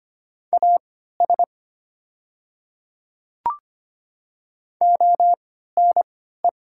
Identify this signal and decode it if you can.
Morse code, heard as a keyed tone, about 25 wpm, spaced as for 14 wpm, 700 Hz. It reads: AS ONE
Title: 3x Character Spacing - Top 200 Words - 25wpm